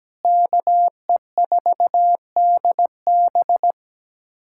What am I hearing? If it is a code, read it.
Morse code, heard as a keyed tone, about 17 wpm, 700 Hz. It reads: KE4DB
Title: USA Callsign DE Callsign - 5 Characters - 17wpm